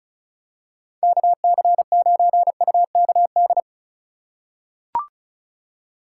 Morse code, heard as a keyed tone, about 36 wpm, 700 Hz. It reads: KC9UKD
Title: USA Call Signs - 3x Character Spacing - 35wpm